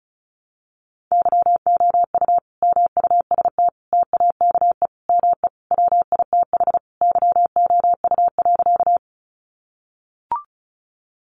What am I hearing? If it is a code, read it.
Morse code, heard as a keyed tone, about 34 wpm, 700 Hz. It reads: YOU MUST TAKE ME WITH YOU.
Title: Sentences from Top 200 Words 35wpm